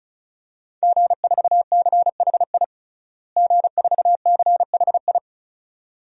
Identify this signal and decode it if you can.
Morse code, heard as a keyed tone, about 34 wpm, 700 Hz. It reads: G4CHI G4CHI
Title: DXCC-100 Call Signs - 5 Characters - Repeated 2x - 35wpm